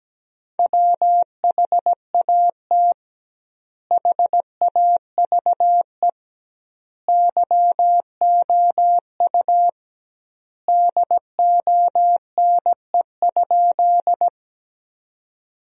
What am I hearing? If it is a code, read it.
Morse code, heard as a keyed tone, about 17 wpm, 700 Hz. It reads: WHAT HAVE YOU DONE?
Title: 2x Word-Spacing - Sentences from Top 500 Words - 17wpm